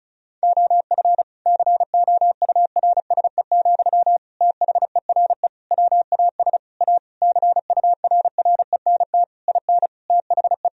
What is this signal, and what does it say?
Morse code, heard as a keyed tone, about 36 wpm, 700 Hz. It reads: OF COURSE, THERE WAS A CURRENT IN THE
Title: Sentences from Top 1000 Words - Review Part 4 - 35wpm